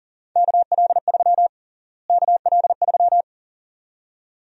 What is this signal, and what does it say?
Morse code, heard as a keyed tone, about 40 wpm, 700 Hz. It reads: KL3 KL3
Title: USA Call Sign Prefixes - Repeated 2x - 40wpm